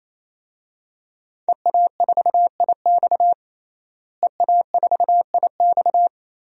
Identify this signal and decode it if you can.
Morse code, heard as a keyed tone, about 28 wpm, 700 Hz. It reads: EA4IX EA4IX
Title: DXCC-100 Call Signs - 5 Characters - Repeated 3x - 28wpm